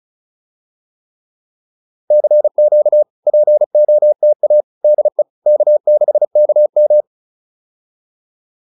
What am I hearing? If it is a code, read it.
Morse code, heard as a keyed tone, about 36 wpm, 600 Hz. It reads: CQ POTA DE K6KM